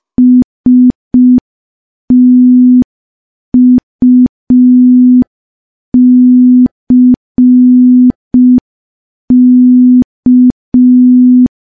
Morse code, heard as a keyed tone, about 5 wpm, 250 Hz. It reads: STUCK
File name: morse / morse-training-05WPM